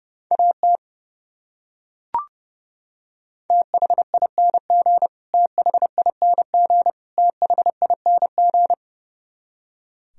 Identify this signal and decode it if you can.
Morse code, heard as a keyed tone, about 30 wpm, 700 Hz. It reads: AT THING THING THING
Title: Top 100 Words Repeated 30wpm